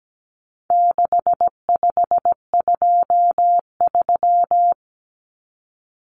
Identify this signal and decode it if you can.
Morse code, heard as a keyed tone, about 17 wpm, 700 Hz. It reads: 6523
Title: Numbers - 4-Digit 17wpm